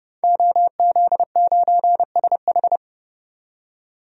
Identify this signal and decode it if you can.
Morse code, heard as a keyed tone, about 32 wpm, 700 Hz. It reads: OZ9SH
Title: DXCC-100 Call Signs - 5 Characters - 30wpm